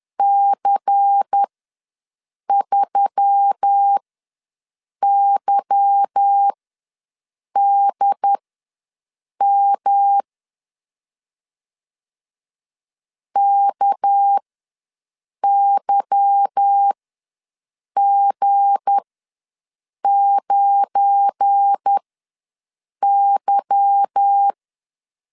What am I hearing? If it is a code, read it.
Morse code, heard as a keyed tone, about 10 wpm, spaced as for 6 wpm, 800 Hz. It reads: C3YDM KYG9Y